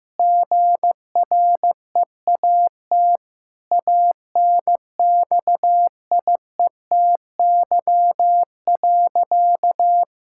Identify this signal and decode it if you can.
Morse code, heard as a keyed tone, about 15 wpm, 700 Hz. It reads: GREAT ANXIETY.
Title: Sentences from Top 1600 Words 15wpm